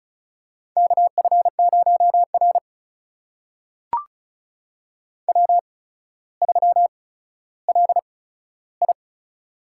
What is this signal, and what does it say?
Morse code, heard as a keyed tone, about 36 wpm, spaced as for 15 wpm, 700 Hz. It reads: KF0R W3LI
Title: USA Call Signs - 4 Characters - 8x Character Spacing - 35wpm